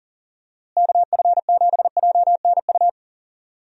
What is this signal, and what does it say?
Morse code, heard as a keyed tone, about 39 wpm, 700 Hz. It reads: KF7JNU
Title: USA Call Signs - 6 Characters - Repeated 2x - 40wpm